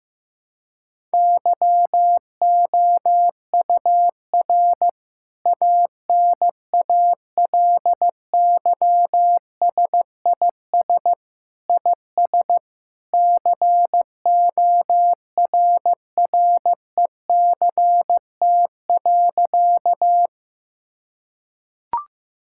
Morse code, heard as a keyed tone, about 15 wpm, 700 Hz. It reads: YOUR ANALYSIS IS CORRECT.
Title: Sentences from Top 1000 Words - Review Part 5 - 15wpm